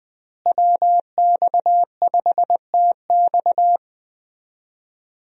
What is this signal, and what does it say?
Morse code, heard as a keyed tone, about 20 wpm, 700 Hz. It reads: WX5TX
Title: USA Call Signs - 5 Characters - Repeated 3x - 20wpm